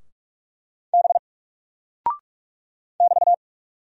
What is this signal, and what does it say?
Morse code, heard as a keyed tone, about 46 wpm, 700 Hz. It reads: B <BT>
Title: Single Character (Letter:Number:Punctuation:Prosign) 45wpm